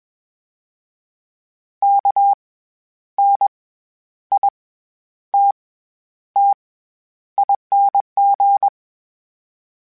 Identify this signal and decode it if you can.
Morse code, heard as a keyed tone, about 21 wpm, 800 Hz. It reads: K N I T T ING
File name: Special INGs-mp3s / Special INGs-mp3s / SlowToFaster ing @20